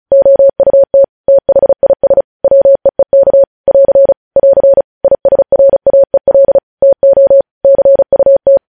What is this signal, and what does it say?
Morse code, heard as a keyed tone, about 36 wpm, 550 Hz. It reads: OUT THIS WEEK <AR> <AR> ISRAEL TO CUT
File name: podcast / News headlines-1x-35x35